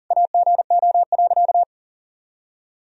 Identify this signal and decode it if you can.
Morse code, heard as a keyed tone, about 40 wpm, 700 Hz. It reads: AGO.
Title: Sentences from Top 1000 Words - Review Part 3 - 40wpm